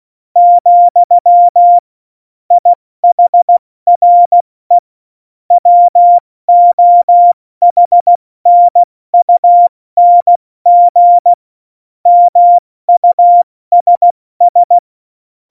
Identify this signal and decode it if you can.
Morse code, heard as a keyed tone, about 16 wpm, 700 Hz. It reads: , IHRE WOHNUNG MUSS
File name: Morsekurs / Witze / Witz 25. Dezember, 70 F80